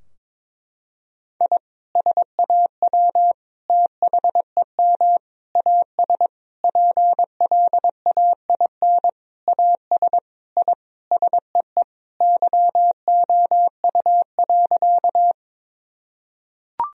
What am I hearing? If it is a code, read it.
Morse code, heard as a keyed tone, about 22 wpm, 700 Hz. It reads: I SAW THEM AS PLAIN AS I SEE YOU.
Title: Sentences from Top 1000 Words - Review Part 2 - 22wpm